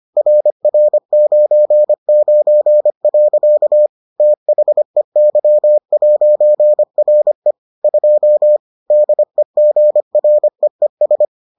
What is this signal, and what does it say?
Morse code, heard as a keyed tone, about 25 wpm, 600 Hz. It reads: RR99. THEY'RE 2 DEGREES